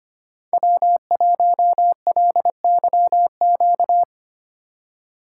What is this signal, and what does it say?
Morse code, heard as a keyed tone, about 25 wpm, 700 Hz. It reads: W1LYQ
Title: USA Call Signs - 4x Character Spacing - 25wpm